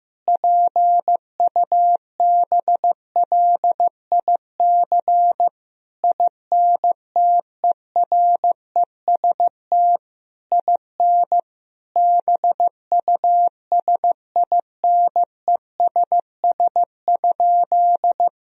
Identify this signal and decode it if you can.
Morse code, heard as a keyed tone, about 15 wpm, 700 Hz. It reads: PUBLIC INTEREST IN BUSINESS?
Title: Sentences from Top 1000 Words - Review Part 3 - 15wpm